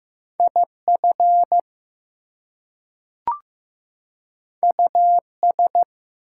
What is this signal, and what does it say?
Morse code, heard as a keyed tone, about 15 wpm, 700 Hz. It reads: IF US
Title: Two Letter Words - 15wpm